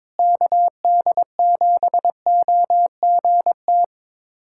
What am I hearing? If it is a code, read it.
Morse code, heard as a keyed tone, about 22 wpm, 700 Hz. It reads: KD7OGT
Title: USA Callsign DE Callsign - 6 Characters - 22wpm